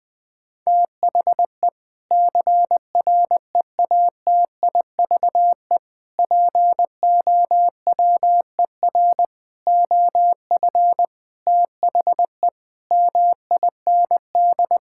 Morse code, heard as a keyed tone, about 20 wpm, 700 Hz. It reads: THE CREATIVE POWER OF THE MIND
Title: Sentences from Top 1000 Words - Review Part 5 - 20wpm